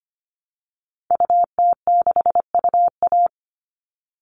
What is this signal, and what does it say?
Morse code, heard as a keyed tone, about 25 wpm, 700 Hz. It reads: UT6UA
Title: Call Signs Repeated 3x - 25wpm